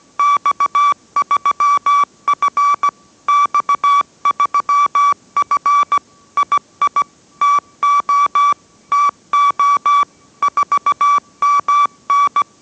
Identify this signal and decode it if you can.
Morse code, heard as a keyed tone, about 17 wpm, 1150 Hz. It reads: X3F X3F II TO TO 4MN